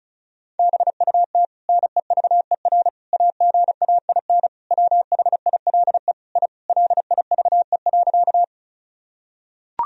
Morse code, heard as a keyed tone, about 36 wpm, 700 Hz. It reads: BUT NEVER AGAIN WHILE I LIVE.
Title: Sentences from Top 1000 Words - Review Part 1 - 35wpm